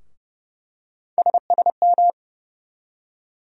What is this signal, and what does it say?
Morse code, heard as a keyed tone, about 30 wpm, 700 Hz. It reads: SSM
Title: USA Call Sign Postfixes - 3 Letters 30wpm